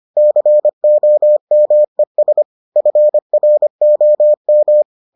Morse code, heard as a keyed tone, about 25 wpm, 600 Hz. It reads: COMES FROM